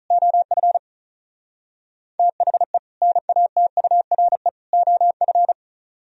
Morse code, heard as a keyed tone, about 42 wpm, 700 Hz. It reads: OF THE NATURE OF
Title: Speed-Racing - Sets of 3 Words 35wpm